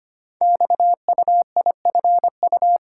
Morse code, heard as a keyed tone, about 25 wpm, 700 Hz. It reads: XUIFU
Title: Code Groups - 5 Letters - 25wpm